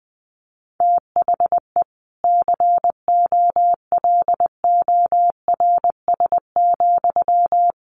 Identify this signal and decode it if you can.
Morse code, heard as a keyed tone, about 20 wpm, 700 Hz. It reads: THE COLORS,
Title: A Sound Of Thunder 20wpm